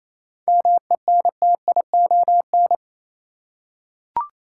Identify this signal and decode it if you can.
Morse code, heard as a keyed tone, about 28 wpm, 700 Hz. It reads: MENTION
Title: Seven Letter Words - 28wpm